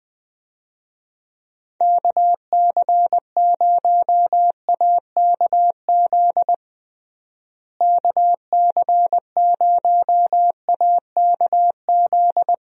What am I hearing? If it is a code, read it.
Morse code, heard as a keyed tone, about 21 wpm, 700 Hz. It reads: KC0AKZ KC0AKZ
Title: USA Call Signs - 6 Characters - Repeated 2x - 20wpm